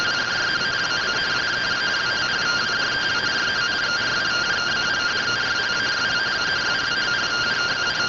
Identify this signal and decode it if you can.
Two-tone teleprinter signal, about 170 Hz shift.